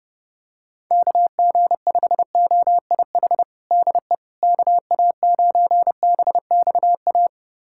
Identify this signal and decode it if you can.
Morse code, heard as a keyed tone, about 30 wpm, 700 Hz. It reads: KG5OIH DE KA9BXA
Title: USA Callsign DE Callsign - 6 Characters - 30wpm